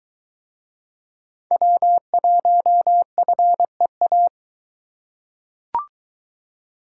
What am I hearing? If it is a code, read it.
Morse code, heard as a keyed tone, about 23 wpm, 700 Hz. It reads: W1FEA